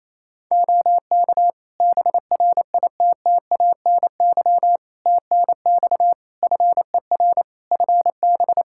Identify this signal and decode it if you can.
Morse code, heard as a keyed tone, about 28 wpm, 700 Hz. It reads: OK BRITTANY TNX FER FB
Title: K1USN SST QSO Practice 28wpm